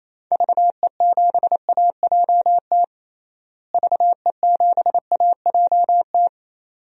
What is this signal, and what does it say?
Morse code, heard as a keyed tone, about 28 wpm, 700 Hz. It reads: VE7AJT VE7AJT
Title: DXCC-100 Call Signs - 6 Characters - Repeated 2x - 28wpm